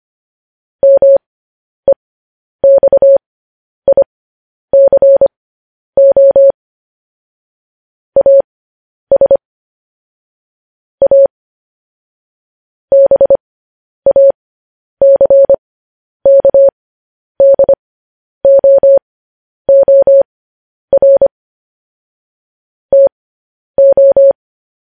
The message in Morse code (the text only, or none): MEXICO AS A BACKDOOR TO